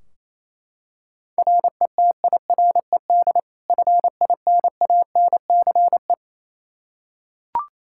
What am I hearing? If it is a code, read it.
Morse code, heard as a keyed tone, about 28 wpm, 700 Hz. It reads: RETIRED FINANCE
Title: Practice QSO - Exchange 3 - More Info - 28wpm